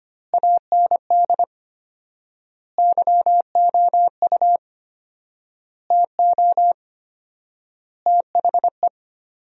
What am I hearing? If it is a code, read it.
Morse code, heard as a keyed tone, about 25 wpm, 700 Hz. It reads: AND YOU TO THE
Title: 4x Word-Spacing - Sentences from Top 400 Words - 25wpm